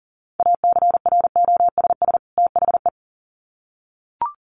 Morse code, heard as a keyed tone, about 40 wpm, 700 Hz. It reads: ACROSS THE
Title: Sets of 2 Words 40wpm